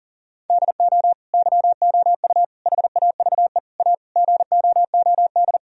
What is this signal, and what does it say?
Morse code, heard as a keyed tone, about 40 wpm, 700 Hz. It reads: DO YOU HAVE A GOOD
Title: Sentences from Top 1000 Words - Review Part 3 - 40wpm